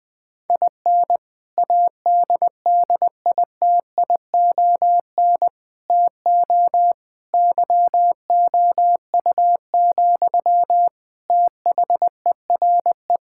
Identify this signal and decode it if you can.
Morse code, heard as a keyed tone, about 20 wpm, 700 Hz. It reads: IN ADDITION TO YOU, THERE
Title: Sentences from Top 1000 Words - Review Part 6 - 20wpm